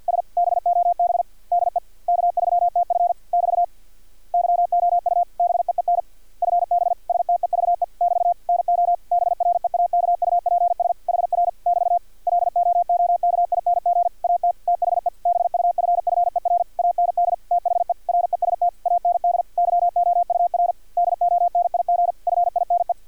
Morse code, heard as a keyed tone, about 50 wpm, 700 Hz. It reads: S BOB DE K3TF <BT> YOU BEEN FB STEVE <BT> NO BREAKUPS HR <BT> LOOKING AT THE BUFFER AND THE REST AND YOUR DONIG FINE
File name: archived / CRSnet